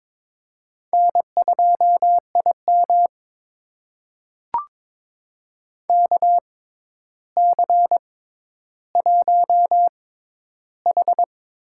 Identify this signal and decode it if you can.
Morse code, heard as a keyed tone, about 22 wpm, spaced as for 11 wpm, 700 Hz. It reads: N2IM KC1H
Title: USA Call Signs - 4 Characters - 6x Character Spacing - 22wpm